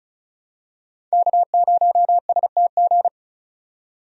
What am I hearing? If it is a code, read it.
Morse code, heard as a keyed tone, about 34 wpm, 700 Hz. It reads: K0STG
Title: USA Call Signs - 5 Characters - 35wpm